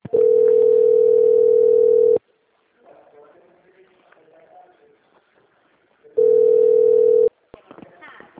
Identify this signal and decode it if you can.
Ringback tone (North America), 440 + 480 Hz, with speech.
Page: Dracena 2 - SP